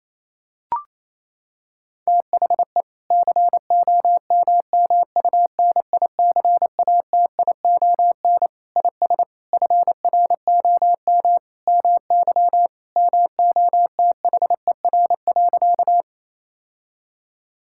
Morse code, heard as a keyed tone, about 27 wpm, 700 Hz. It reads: THE COMMUNICATION IS FROM MY MOTHER.
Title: Sentences from Top 1000 Words - Review Part 6 - 28wpm